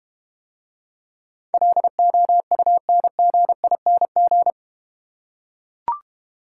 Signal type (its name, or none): Morse code